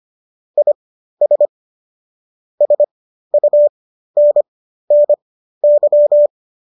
Morse code, heard as a keyed tone, about 25 wpm, spaced as for 13 wpm, 590 Hz. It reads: IS SUNNY